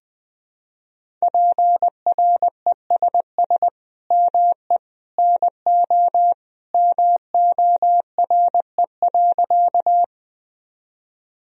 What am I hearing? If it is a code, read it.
Morse code, heard as a keyed tone, about 20 wpm, 700 Hz. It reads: PRESS ME NO MORE.
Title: Sentences from Top 1000 Words - Review Part 1 - 20wpm